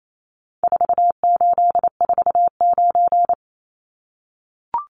Morse code, heard as a keyed tone, about 28 wpm, 700 Hz. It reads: 4849
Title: Numbers - 4-Digit 28wpm